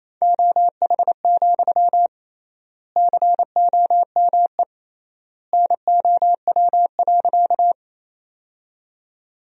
Morse code, heard as a keyed tone, about 27 wpm, 700 Hz. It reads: OH, COME NOW.
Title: 3x Word-Spacing - Sentences from Top 500 Words - 28wpm